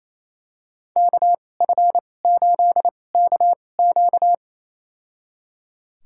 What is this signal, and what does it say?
Morse code, heard as a keyed tone, about 27 wpm, spaced as for 20 wpm, 700 Hz. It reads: KF8KQ
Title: USA Call Signs - 2x Character Spacing - 28wpm